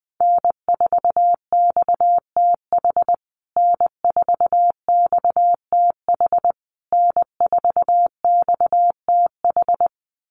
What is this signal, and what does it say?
Morse code, heard as a keyed tone, about 20 wpm, 700 Hz. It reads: N4XTH N4XTH N4XTH
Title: USA Call Signs Repeated 3x 20wpm